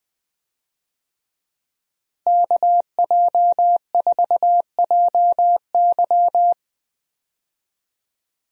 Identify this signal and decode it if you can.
Morse code, heard as a keyed tone, about 20 wpm, 700 Hz. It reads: KJ4JY
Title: USA Call Signs - 5 Characters - Repeated 2x - 20wpm